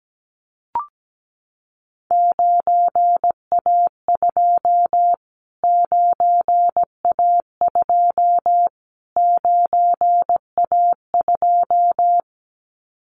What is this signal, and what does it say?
Morse code, heard as a keyed tone, about 17 wpm, 700 Hz. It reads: 9A2 9A2 9A2
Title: Callsign Prefixes for Top 25 DXCC - Repeated 3x - 17wpm